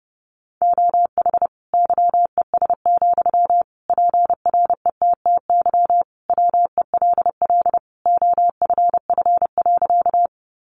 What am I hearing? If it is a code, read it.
Morse code, heard as a keyed tone, about 30 wpm, 700 Hz. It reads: OH YES, PRETTY WELL OFF.
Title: Sentences from Top 1000 Words 30wpm